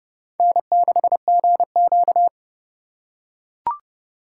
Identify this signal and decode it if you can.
Morse code, heard as a keyed tone, about 28 wpm, 700 Hz. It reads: N6GQ